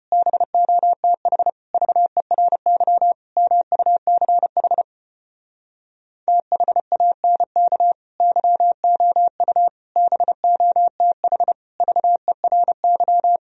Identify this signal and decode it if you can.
Morse code, heard as a keyed tone, about 33 wpm, 700 Hz. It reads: BOTH VERY MUCH THANK YOU BOTH VERY